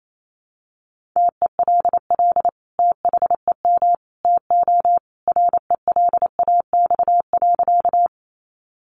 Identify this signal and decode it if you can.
Morse code, heard as a keyed tone, about 28 wpm, 700 Hz. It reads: TELL THEM TO RELAX.
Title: Sentences from Top 1400 Words 28wpm